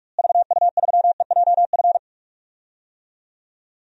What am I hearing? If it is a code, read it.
Morse code, heard as a keyed tone, about 46 wpm, 700 Hz. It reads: VU3EJF
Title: DXCC-100 Call Signs - 6 Characters - Repeated 2x - 45wpm